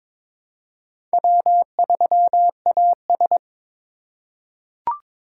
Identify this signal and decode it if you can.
Morse code, heard as a keyed tone, about 22 wpm, 700 Hz. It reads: W3AS
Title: USA Call Signs - 4 Characters - 3x Character Spacing - 22wpm